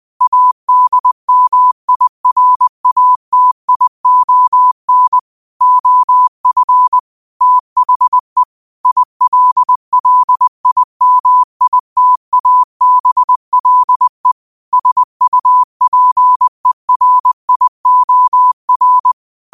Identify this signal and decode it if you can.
Morse code, heard as a keyed tone, about 20 wpm, 1000 Hz. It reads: ADMIRATION OF THE ILLIMITABLE SUPERIOR
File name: qotdfiles / 20wpm